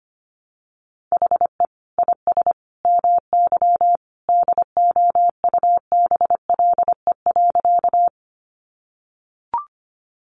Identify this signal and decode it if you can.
Morse code, heard as a keyed tone, about 25 wpm, 700 Hz. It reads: HE IS MY DOUBLE.
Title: Sentences from Top 1400 Words 25wpm